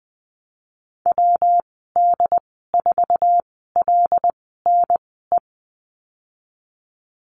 Morse code, heard as a keyed tone, about 20 wpm, spaced as for 15 wpm, 700 Hz. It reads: WD4LNE
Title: 167 Lesson 27g - Number 7 - Call Signs 20wpm